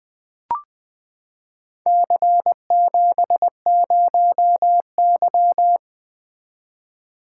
Morse code, heard as a keyed tone, about 20 wpm, 700 Hz. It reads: C70Y
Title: Four Letter-Number Combinations - 20wpm